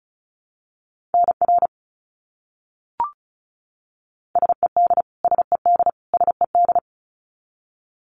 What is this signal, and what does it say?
Morse code, heard as a keyed tone, about 34 wpm, 700 Hz. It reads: NR SED SED SED
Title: Abbreviations Repeated 3x - 35wpm